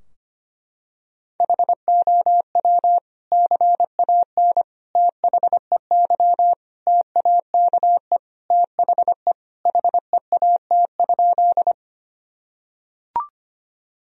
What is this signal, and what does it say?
Morse code, heard as a keyed tone, about 25 wpm, 700 Hz. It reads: HOW CAN THEY TAKE THE HEAT?
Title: Sentences from Top 1000 Words - Review Part 3 - 25wpm